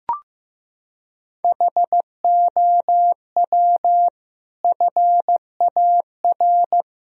Morse code, heard as a keyed tone, about 15 wpm, 700 Hz. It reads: HOW FAR
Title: Sentences from Top 1000 Words - Review Part 4 - 15wpm